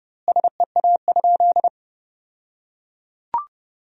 Morse code, heard as a keyed tone, about 30 wpm, 700 Hz. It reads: SEA?
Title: 2x Word-Spacing - Sentences from Top 300 Words - 30wpm